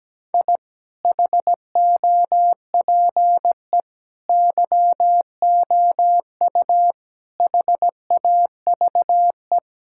Morse code, heard as a keyed tone, about 17 wpm, 700 Hz. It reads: I HOPE YOU HAVE
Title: Sentences from Top 1000 Words - Review Part 6 - 17wpm